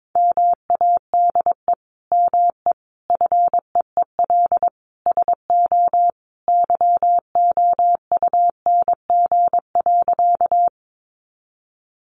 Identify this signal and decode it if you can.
Morse code, heard as a keyed tone, about 22 wpm, 700 Hz. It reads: MADE ME FEEL SO YOUNG.
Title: Sentences from Top 400 Words 22wpm